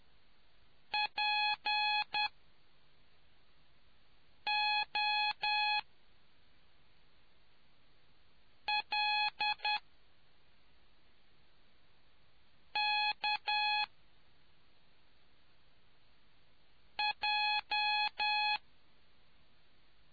Morse code, heard as a keyed tone, about 10 wpm, spaced as for 3 wpm, 840 Hz. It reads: POLKJ